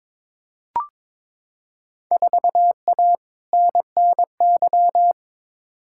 Morse code, heard as a keyed tone, about 22 wpm, 700 Hz. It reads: 4A NNY
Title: Field Day Exchanges - 22wpm